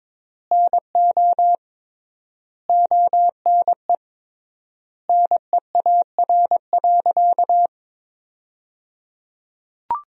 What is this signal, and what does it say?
Morse code, heard as a keyed tone, about 22 wpm, 700 Hz. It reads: NO ONE NEAR.
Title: 3x Word-Spacing - Sentences from Top 200 Words - 22wpm